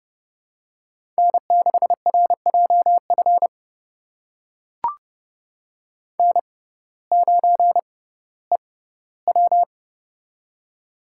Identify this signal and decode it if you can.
Morse code, heard as a keyed tone, about 31 wpm, spaced as for 17 wpm, 700 Hz. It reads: N6RJF N9EW
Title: USA Call Signs - 6x Character Spacing - 30wpm